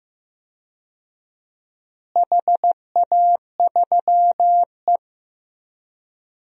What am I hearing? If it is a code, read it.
Morse code, heard as a keyed tone, about 15 wpm, 700 Hz. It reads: HA3E
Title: DXCC-100 Call Signs - 4 Characters - Repeated 2x - 15wpm